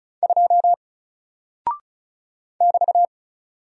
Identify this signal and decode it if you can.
Morse code, heard as a keyed tone, about 36 wpm, 700 Hz. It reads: 2 <BT>
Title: Single Character (Letter:Number:Punctuation:Prosign) 35wpm